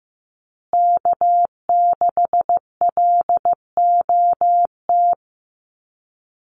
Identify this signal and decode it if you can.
Morse code, heard as a keyed tone, about 15 wpm, 700 Hz. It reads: K6LOT
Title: USA Call Signs Repeated 2x 15wpm